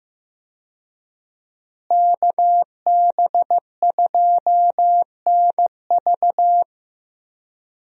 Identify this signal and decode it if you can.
Morse code, heard as a keyed tone, about 15 wpm, 700 Hz. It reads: KB2NV
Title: USA Call Signs - 5 Characters - Repeated 2x - 15wpm